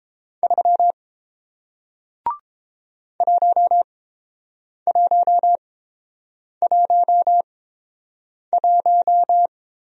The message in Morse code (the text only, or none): 3 1111